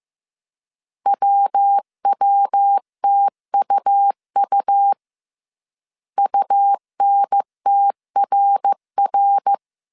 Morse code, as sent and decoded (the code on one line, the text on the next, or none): .-- .-- - ..- ..- / ..- -. - .-. .-.
WWTUU UNTRR